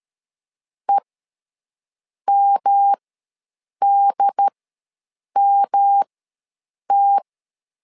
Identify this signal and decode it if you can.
Morse code, heard as a keyed tone, about 13 wpm, spaced as for 7 wpm, 800 Hz. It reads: EMDMT
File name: LEZIONI / 3 MWRO / 3 M ws 12wpm 1745